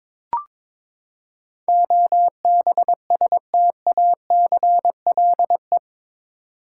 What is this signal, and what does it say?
Morse code, heard as a keyed tone, about 22 wpm, 700 Hz. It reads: OBSTACLE
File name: practice / 22wpm / Eight Letter Words - 22wpm